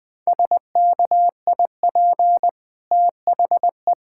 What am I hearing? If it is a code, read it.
Morse code, heard as a keyed tone, about 20 wpm, 700 Hz. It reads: SKIP THE
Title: Easy Sentences with Hints - 20wpm